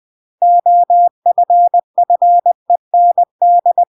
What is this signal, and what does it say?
Morse code, heard as a keyed tone, about 20 wpm, 700 Hz. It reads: OFFEND
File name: CW / jodorowsky / 82 commandments 20-20